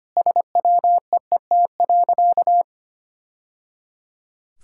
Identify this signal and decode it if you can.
Morse code, heard as a keyed tone, about 25 wpm, 700 Hz. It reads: SWEET.
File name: practice / 25wpm / Sentences with 3-Word Phrases - 25wpm